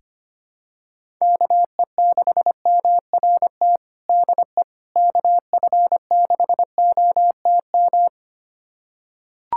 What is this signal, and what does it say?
Morse code, heard as a keyed tone, about 25 wpm, 700 Hz. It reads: KE6MRT DE KF6OTM
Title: USA Callsign DE Callsign - 6 Characters - 25wpm